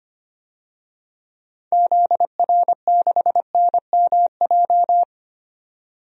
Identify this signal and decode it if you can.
Morse code, heard as a keyed tone, about 25 wpm, 690 Hz. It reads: ZR6NMJ
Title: DXCC-100 Call Signs - 6 Characters - 25wpm